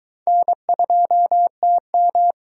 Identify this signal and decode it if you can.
Morse code, heard as a keyed tone, about 22 wpm, 700 Hz. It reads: N2TM